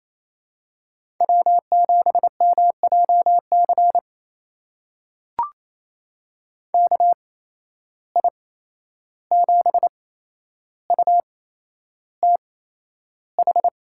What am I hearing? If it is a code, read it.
Morse code, heard as a keyed tone, about 28 wpm, spaced as for 11 wpm, 700 Hz. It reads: W7MJC KI7UTH
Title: USA Call Signs - 8x Character Spacing - 28wpm